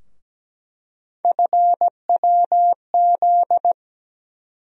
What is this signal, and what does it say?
Morse code, heard as a keyed tone, about 17 wpm, 700 Hz. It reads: FWZ
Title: USA Call Sign Postfixes - 3 Letters 17wpm